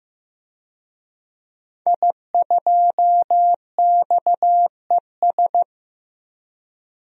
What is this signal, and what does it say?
Morse code, heard as a keyed tone, about 15 wpm, 700 Hz. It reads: I2XES
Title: DXCC-100 Call Signs - 5 Characters - 15wpm